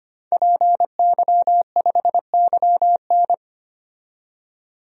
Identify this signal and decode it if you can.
Morse code, heard as a keyed tone, about 25 wpm, 700 Hz. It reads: PY5YN
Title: DXCC-100 Call Signs - 5 Characters - 25wpm